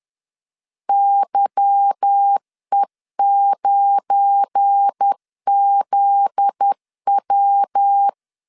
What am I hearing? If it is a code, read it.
Morse code, heard as a keyed tone, about 10 wpm, 800 Hz. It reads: YE9ZW
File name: LEZIONI / 8 YJ45 / 8 Y 10wpm 2615